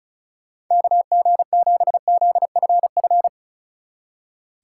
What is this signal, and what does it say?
Morse code, heard as a keyed tone, about 44 wpm, 700 Hz. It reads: KG7ZFF